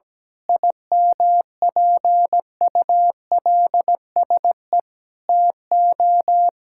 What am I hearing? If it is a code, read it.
Morse code, heard as a keyed tone, about 17 wpm, 700 Hz. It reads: IMPULSE TO
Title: Pangram Challenge v2 - 17wpm